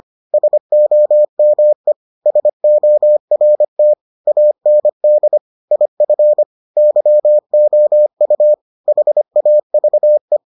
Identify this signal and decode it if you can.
Morse code, heard as a keyed tone, about 25 wpm, 600 Hz. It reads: SOME SORT AND IF YOU HAVE